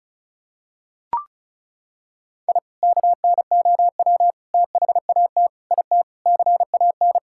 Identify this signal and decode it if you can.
Morse code, heard as a keyed tone, about 36 wpm, 700 Hz. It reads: I KNOW THAT IT CAN
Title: Sentences from Top 1000 Words - Review Part 6 - 35wpm